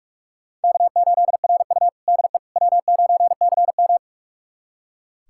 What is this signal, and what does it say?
Morse code, heard as a keyed tone, about 43 wpm, 700 Hz. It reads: K8RU DE W9CM
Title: USA Callsign DE Callsign - 4 Characters - 45wpm